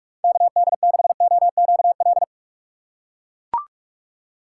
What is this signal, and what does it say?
Morse code, heard as a keyed tone, about 44 wpm, 700 Hz. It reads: KD6OQL
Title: USA Call Signs - 6 Characters - 6x Character Spacing - 45wpm